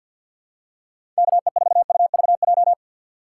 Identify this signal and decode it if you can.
Morse code, heard as a keyed tone, about 53 wpm, 700 Hz. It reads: KE4UVJ
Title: USA Call Signs - 6 Characters - 4x Character Spacing - 50wpm